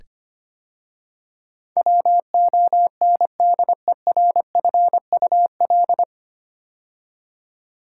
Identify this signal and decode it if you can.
Morse code, heard as a keyed tone, about 25 wpm, 690 Hz. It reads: WONDERFUL
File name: practice / 25wpm / 9 Letter Words - First 3 Letters Given - 25wpm